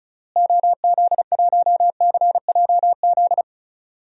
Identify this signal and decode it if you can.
Morse code, heard as a keyed tone, about 36 wpm, 700 Hz. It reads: OZ1CJZ
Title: DXCC-100 Call Signs - 6 Characters - Repeated 3x - 35wpm